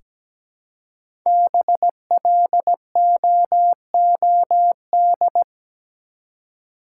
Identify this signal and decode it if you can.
Morse code, heard as a keyed tone, about 17 wpm, 700 Hz. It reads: BLOOD
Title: Five Letter Words - 17wpm